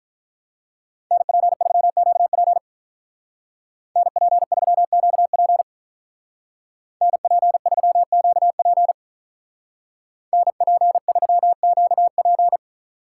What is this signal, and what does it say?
Morse code, heard as a keyed tone, about 43 wpm, 700 Hz. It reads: NP3QP NP3QP NP3QP NP3QP
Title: Speed Racing - USA 2x2 Call Signs 35wpm